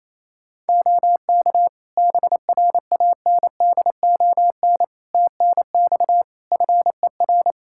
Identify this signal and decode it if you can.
Morse code, heard as a keyed tone, about 28 wpm, 700 Hz. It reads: OK BRANDON TNX FER
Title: K1USN SST QSO Practice 28wpm